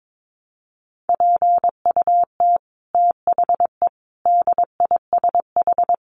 Morse code, heard as a keyed tone, about 22 wpm, 700 Hz. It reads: PUT THE DISH
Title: Sentences from Top 1300 Words 22wpm